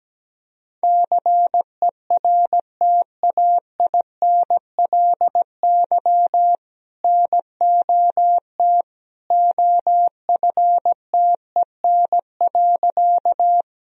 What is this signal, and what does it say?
Morse code, heard as a keyed tone, about 17 wpm, 700 Hz. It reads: CERTAINLY NOT OFTEN.
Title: Sentences from Top 1000 Words - Review Part 1 - 17wpm